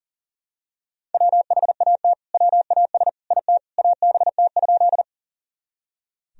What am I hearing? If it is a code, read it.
Morse code, heard as a keyed tone, about 40 wpm, 700 Hz. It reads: WHAT WAS IT ABT?
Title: Sentences from Top 100 Words and Abbreviations 40wpm